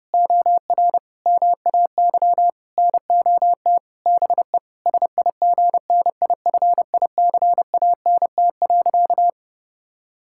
Morse code, heard as a keyed tone, about 30 wpm, 700 Hz. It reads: OR MAY NOT BE SIGNIFICANT.
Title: Sentences from Top 1000 Words - Review Part 6 - 30wpm